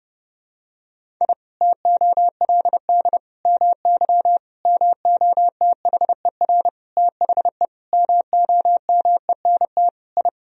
Morse code, heard as a keyed tone, about 31 wpm, 700 Hz. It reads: I TOLD MY MOTHER THE MOMENT I
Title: Sentences from Top 1000 Words - Review Part 1 - 30wpm